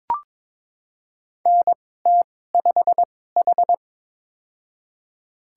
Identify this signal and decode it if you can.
Morse code, heard as a keyed tone, about 21 wpm, spaced as for 16 wpm, 700 Hz. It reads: NT5H
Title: USA Call Signs - 4 Characters - 2x Character Spacing - 22wpm